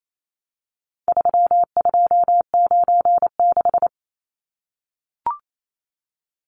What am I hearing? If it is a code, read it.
Morse code, heard as a keyed tone, about 28 wpm, 700 Hz. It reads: 3296
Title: Numbers - 4-Digit 28wpm